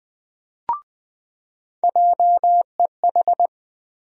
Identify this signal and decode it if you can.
Morse code, heard as a keyed tone, about 21 wpm, 700 Hz. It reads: JEH
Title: USA Call Sign Postfixes - 3 Letters 20wpm